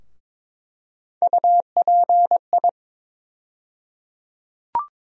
Morse code, heard as a keyed tone, about 22 wpm, 700 Hz. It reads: UPI
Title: USA Call Sign Postfixes - 3 Letters 22wpm